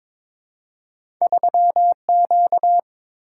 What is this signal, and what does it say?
Morse code, heard as a keyed tone, about 22 wpm, 700 Hz. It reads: 3Q
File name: practice / 22wpm / Two Letter-Number Combinations - 22wpm